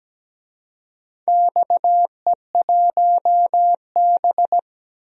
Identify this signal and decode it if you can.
Morse code, heard as a keyed tone, about 17 wpm, 700 Hz. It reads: XE1B
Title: DXCC-100 Call Signs - 4 Characters - Repeated 2x - 17wpm